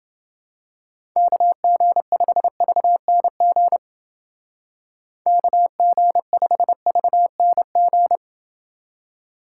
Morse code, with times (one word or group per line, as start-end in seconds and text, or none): 1.2-3.8 KG5VNG
5.3-8.2 KG5VNG